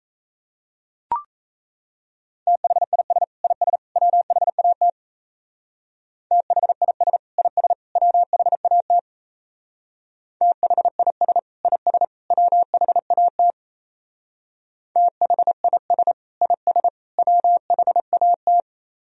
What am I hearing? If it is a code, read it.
Morse code, heard as a keyed tone, about 32 wpm, 700 Hz. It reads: THIS IS WHAT THIS IS WHAT THIS IS WHAT THIS IS WHAT
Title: Speed-Racing - Sets of 3 Words 28wpm